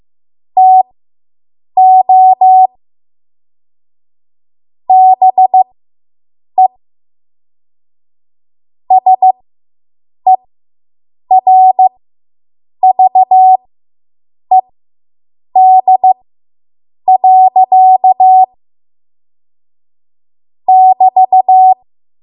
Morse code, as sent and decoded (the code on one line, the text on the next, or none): - --- / -... / ... .-. ...- . -.. .-.-.- / -...-
TO BE SERVED. <BT>